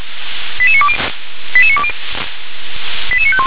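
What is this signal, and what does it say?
Five-tone selective call (ZVEI), heard as quick runs of five tones, tones of about 70 ms. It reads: 88920 88920 88951
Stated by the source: Five-tone ZVEI selective call received on SDR.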